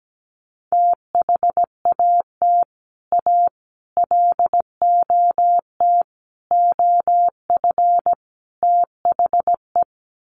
Morse code, heard as a keyed tone, about 17 wpm, 700 Hz. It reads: THAT A LOT OF THE
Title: Sets of 5 Words Encore2 17wpm